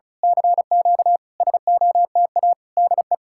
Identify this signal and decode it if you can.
Morse code, heard as a keyed tone, about 32 wpm, 700 Hz. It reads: CQ SOTA DE
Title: Speed-Racing Common QSO Phrases Encore - 35wpm